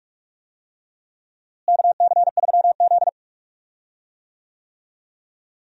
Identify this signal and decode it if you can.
Morse code, heard as a keyed tone, about 46 wpm, 700 Hz. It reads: KC3Z
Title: USA Call Signs - 4 Characters - Repeated 3x - 45wpm